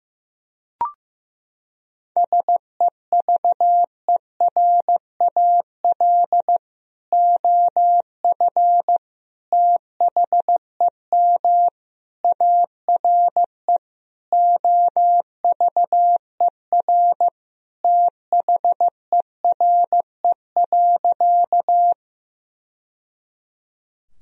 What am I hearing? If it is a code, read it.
Morse code, heard as a keyed tone, about 15 wpm, 700 Hz. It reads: SEVERAL OF THEM ARE OVER THERE.
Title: Sentences from Top 1000 Words - Review Part 2 - 15wpm